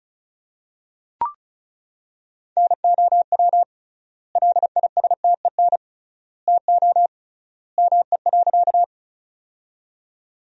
Morse code, heard as a keyed tone, about 36 wpm, 700 Hz. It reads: NOW LISTEN TO ME.
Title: 3x Word-Spacing - Sentences from Top 400 Words - 35wpm